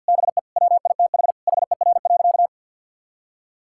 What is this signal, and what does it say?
Morse code, heard as a keyed tone, about 50 wpm, 700 Hz. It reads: BE WITH HER.
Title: Sentences from Top 1000 Words - Review Part 6 - 50wpm